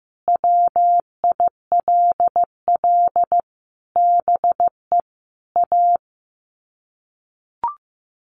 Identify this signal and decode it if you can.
Morse code, heard as a keyed tone, about 15 wpm, 700 Hz. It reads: WILL BE A
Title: Sets of 5 Words Encore2 15wpm